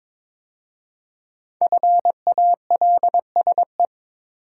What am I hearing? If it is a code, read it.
Morse code, heard as a keyed tone, about 23 wpm, 700 Hz. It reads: FALSE
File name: 5 Letter Words - First 3 Letters Given - 22wpm